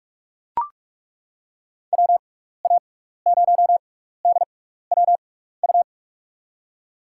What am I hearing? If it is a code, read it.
Morse code, heard as a keyed tone, about 42 wpm, spaced as for 15 wpm, 700 Hz. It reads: WA0DWU
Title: USA Call Signs - 6 Characters - 6x Character Spacing - 45wpm